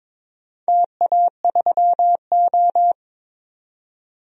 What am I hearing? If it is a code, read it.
Morse code, heard as a keyed tone, about 22 wpm, 700 Hz. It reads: TA3O
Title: DXCC-100 Call Signs - 4 Characters - 22wpm